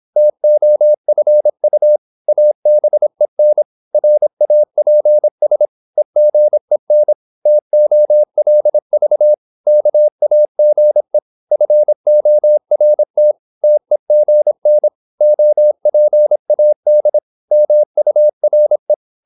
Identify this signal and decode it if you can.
Morse code, heard as a keyed tone, about 26 wpm, 600 Hz. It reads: TOFU ABEN RAPS EGEN TOLV KAGE FORT TEGN OPAD MURE
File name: Korte ord DK 26wpm